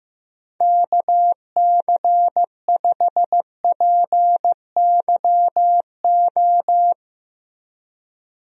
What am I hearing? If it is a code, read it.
Morse code, heard as a keyed tone, about 15 wpm, 700 Hz. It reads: KC5PYO
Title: USA Call Signs - 6 Characters - Repeated 3x - 15wpm